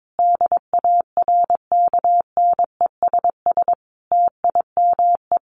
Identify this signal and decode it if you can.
Morse code, heard as a keyed tone, about 22 wpm, 700 Hz. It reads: DARKNESS TIME
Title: A Sound Of Thunder 22wpm